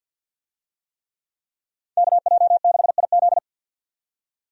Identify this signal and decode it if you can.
Morse code, heard as a keyed tone, about 51 wpm, 700 Hz. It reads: KJ6IZ
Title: USA Call Signs - 5 Characters - 50wpm